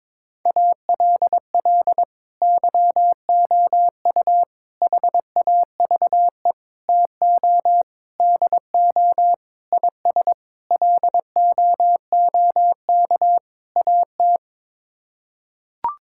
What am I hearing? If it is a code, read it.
Morse code, heard as a keyed tone, about 21 wpm, 700 Hz. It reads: ALL YOU HAVE TO DO IS LOOK AT
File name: practice / 22wpm / Sets of 8 Words - 22wpm